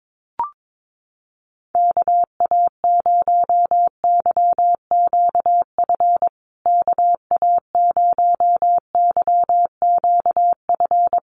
Morse code, heard as a keyed tone, about 22 wpm, 700 Hz. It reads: KA0YQF KA0YQF
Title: USA Call Signs Repeated 2x 22wpm